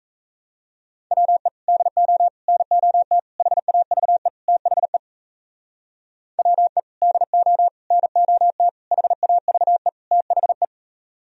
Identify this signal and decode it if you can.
Morse code, heard as a keyed tone, about 37 wpm, 700 Hz. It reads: WE DO NOT HAVE THE WE DO NOT HAVE THE